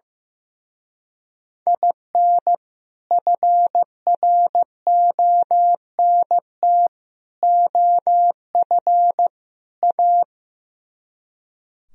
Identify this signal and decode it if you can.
Morse code, heard as a keyed tone, about 17 wpm, 700 Hz. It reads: IN FRONT OF A
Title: Speed-Racing - Sets of 4 Words 15wpm